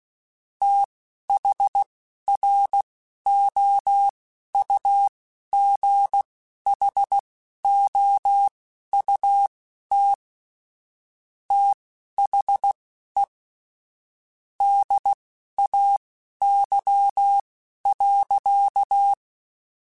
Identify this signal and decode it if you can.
Morse code, heard as a keyed tone, about 16 wpm, spaced as for 12 wpm, 800 Hz. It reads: THROUGHOUT THE DAY.
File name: Weather Sets / Weather Sets-mp3s / Weather 5 12 @10